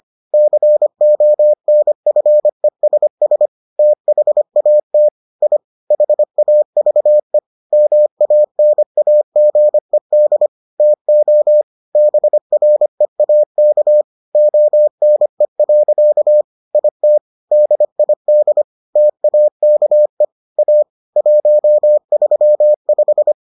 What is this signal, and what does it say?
Morse code, heard as a keyed tone, about 25 wpm, 600 Hz. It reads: CONFESS THAT I HAVE MANAGED TO BREAK ONE. IT DID TAKE A 135